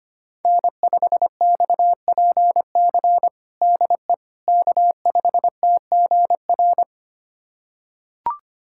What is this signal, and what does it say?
Morse code, heard as a keyed tone, about 25 wpm, 700 Hz. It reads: N5XPC DE K5TGR
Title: USA Callsign DE Callsign - 5 Characters - 25wpm